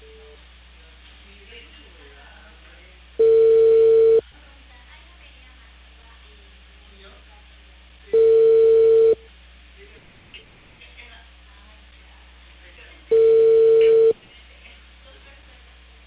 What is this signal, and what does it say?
Ringback tone (China, 450 Hz), with speech.